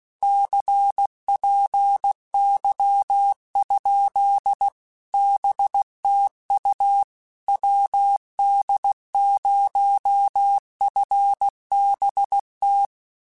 Morse code, heard as a keyed tone, about 16 wpm, 800 Hz. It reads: CPY? BTU WD0FBT